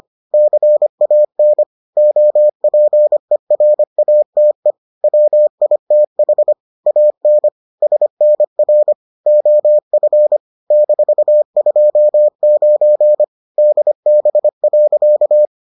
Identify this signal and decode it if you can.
Morse code, heard as a keyed tone, about 25 wpm, 600 Hz. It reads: CAN OPERATE WITH AN SNR OF -29 DB.